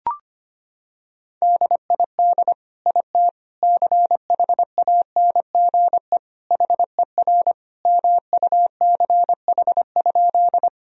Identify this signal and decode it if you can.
Morse code, heard as a keyed tone, about 25 wpm, 700 Hz. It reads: DID IT CHANGE HER MUCH?
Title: Sentences from Top 1000 Words - Review Part 1 - 25wpm